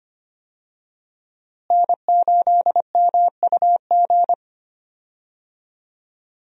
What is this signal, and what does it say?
Morse code, heard as a keyed tone, about 25 wpm, 700 Hz. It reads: N8MUG
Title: USA Call Signs - 5 Characters - Repeated 3x - 25wpm